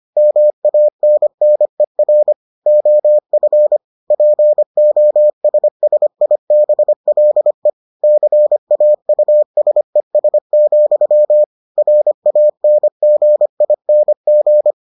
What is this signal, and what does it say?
Morse code, heard as a keyed tone, about 25 wpm, 600 Hz. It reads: MANNER OF POSSIBLE CAUSES, RANGING